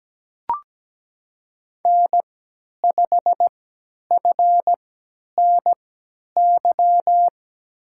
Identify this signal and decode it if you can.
Morse code, heard as a keyed tone, about 17 wpm, spaced as for 10 wpm, 700 Hz. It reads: N5FNY
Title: USA Call Signs - 3x Character Spacing - 17wpm